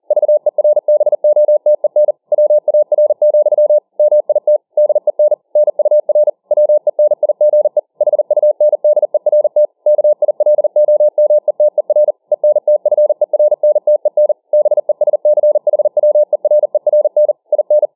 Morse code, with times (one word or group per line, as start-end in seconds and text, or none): 0.1-2.1 VERBOTEN
2.3-3.8 WAR,
4.0-4.6 MIT
4.8-5.3 DEN
5.6-6.3 NUR
6.5-7.8 WENIGE
8.0-9.7 HUNDERT
9.9-12.1 KILOMETER
12.3-14.3 ENTFERNTEN
14.5-17.3 BESCHWERERN
17.5-17.9 IN